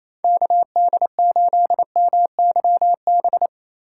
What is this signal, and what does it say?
Morse code, heard as a keyed tone, about 28 wpm, 700 Hz. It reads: KD8MYB
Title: USA Call Signs - 2x Character Spacing - 28wpm